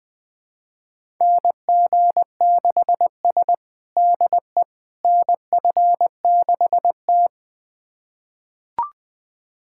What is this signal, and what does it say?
Morse code, heard as a keyed tone, about 20 wpm, 700 Hz. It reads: NG6S DE NF6T
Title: USA Callsign DE Callsign - 4 Characters - 20wpm